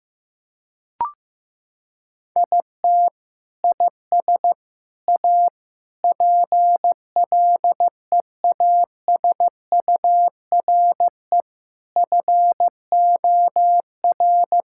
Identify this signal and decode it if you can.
Morse code, heard as a keyed tone, about 16 wpm, 700 Hz. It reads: IT IS A PLEASURE FOR
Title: Sentences from Top 1600 Words 15wpm